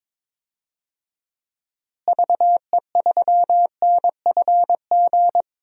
Morse code, heard as a keyed tone, about 22 wpm, 700 Hz. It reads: VE3NFG
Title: DXCC-100 Call Signs - 6 Characters - Repeated 2x - 22wpm